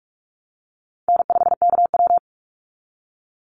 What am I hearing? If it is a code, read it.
Morse code, heard as a keyed tone, about 44 wpm, 700 Hz. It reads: N5KW
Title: Call Signs Repeated 3x - 45wpm